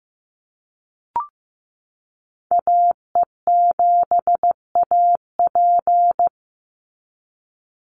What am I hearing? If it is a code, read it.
Morse code, heard as a keyed tone, about 15 wpm, 700 Hz. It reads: AE7AP
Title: NA SOTA Activator Callsigns 15wpm